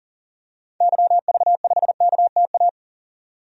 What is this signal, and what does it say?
Morse code, heard as a keyed tone, about 40 wpm, 700 Hz. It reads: YV5KTA
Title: DXCC-100 Call Signs - 6 Characters - 40wpm